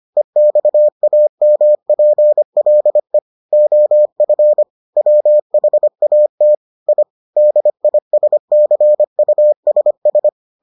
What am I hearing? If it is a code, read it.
Morse code, heard as a keyed tone, about 25 wpm, 600 Hz. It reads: EXAMPLE OF WHAT I DISCUSS